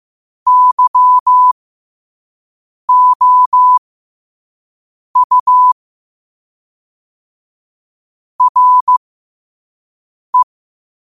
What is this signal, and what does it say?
Morse code, heard as a keyed tone, about 15 wpm, spaced as for 5 wpm, 1000 Hz. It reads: YOU RE
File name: qotdfiles / 5wpm